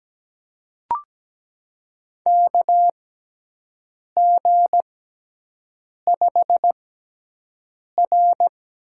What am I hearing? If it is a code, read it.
Morse code, heard as a keyed tone, about 17 wpm, spaced as for 6 wpm, 700 Hz. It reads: KG5R
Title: USA Call Signs - 6x Character Spacing - 17wpm